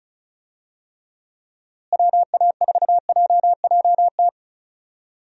Morse code, heard as a keyed tone, about 36 wpm, 700 Hz. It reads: WA4JJT
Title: USA Call Signs - 6 Characters - Repeated 2x - 35wpm